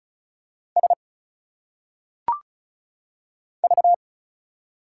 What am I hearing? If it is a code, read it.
Morse code, heard as a keyed tone, about 36 wpm, 700 Hz. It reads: S V
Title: Single Letters - Phonetic Pronunciation 35wpm